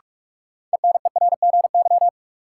Morse code, heard as a keyed tone, about 44 wpm, 700 Hz. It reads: ENERGY
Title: Sentences from Top 900 Words - New Word Drill - 45wpm